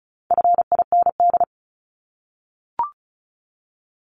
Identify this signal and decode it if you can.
Morse code, heard as a keyed tone, about 34 wpm, 700 Hz. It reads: FIND
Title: ING Suffix - Top 100 Words with Base Words 35wpm